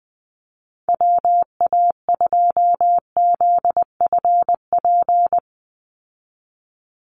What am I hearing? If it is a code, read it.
Morse code, heard as a keyed tone, about 20 wpm, 700 Hz. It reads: WA2ZFP
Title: USA Call Signs 20wpm